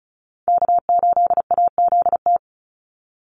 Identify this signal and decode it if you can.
Morse code, heard as a keyed tone, about 34 wpm, 700 Hz. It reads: K8AZT
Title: Call Signs Repeated 3x - 35wpm